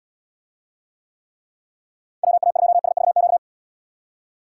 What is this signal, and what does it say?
Morse code, heard as a keyed tone, about 78 wpm, 700 Hz. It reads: WA2SVQ